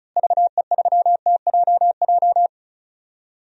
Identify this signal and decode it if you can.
Morse code, heard as a keyed tone, about 34 wpm, 700 Hz. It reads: VE3TJJ
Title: DXCC-100 Call Signs - 6 Characters - Repeated 3x - 35wpm